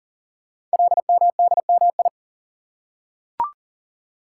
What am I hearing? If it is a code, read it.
Morse code, heard as a keyed tone, about 40 wpm, 700 Hz. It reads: LMDMI
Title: Code Groups - 5 Letters - 40wpm